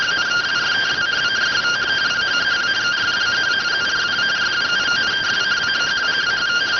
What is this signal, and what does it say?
Two-tone teleprinter signal, about 170 Hz shift.